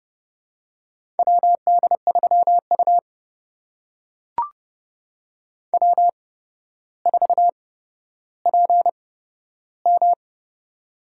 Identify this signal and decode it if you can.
Morse code, heard as a keyed tone, about 30 wpm, spaced as for 13 wpm, 700 Hz. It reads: WD3U W4PM
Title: USA Call Signs - 4 Characters - 8x Character Spacing - 30wpm